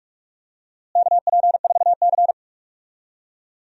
Morse code, heard as a keyed tone, about 46 wpm, 700 Hz. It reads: KP4C
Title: DXCC-100 Call Signs - 4 Characters - Repeated 3x - 45wpm